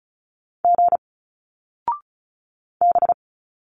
Morse code, heard as a keyed tone, about 36 wpm, 700 Hz. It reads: G B
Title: Single Letters 35wpm